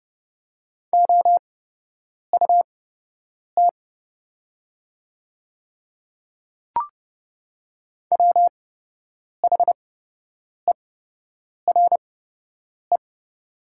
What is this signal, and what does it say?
Morse code, heard as a keyed tone, about 30 wpm, spaced as for 8 wpm, 700 Hz. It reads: OUT WHERE
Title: 8x Character Spacing - Top 200 Words - 30wpm